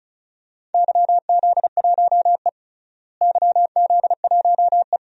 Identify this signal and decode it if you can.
Morse code, heard as a keyed tone, about 36 wpm, 700 Hz. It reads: YZ1E YZ1E
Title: DXCC-100 Call Signs - 4 Characters - Repeated 2x - 35wpm